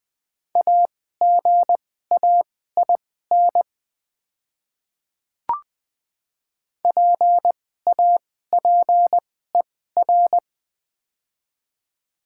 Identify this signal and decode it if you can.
Morse code, heard as a keyed tone, about 20 wpm, spaced as for 15 wpm, 690 Hz. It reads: AGAIN PAPER
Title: 2x Character Spacing - Top 500 Words - 20wpm